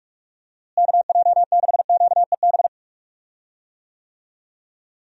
Morse code, heard as a keyed tone, about 41 wpm, 700 Hz. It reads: KJ6QEB
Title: USA Call Signs - 6 Characters - Repeated 2x - 45wpm